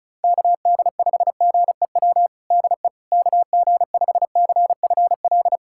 Morse code, heard as a keyed tone, about 36 wpm, 700 Hz. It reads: KD5GEW DE KG5CFL
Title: USA Callsign DE Callsign - 6 Characters - 35wpm